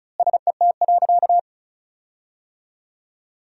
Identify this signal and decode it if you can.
Morse code, heard as a keyed tone, about 36 wpm, 700 Hz. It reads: SET.
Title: 3x Word-Spacing - Sentences from Top 400 Words - 35wpm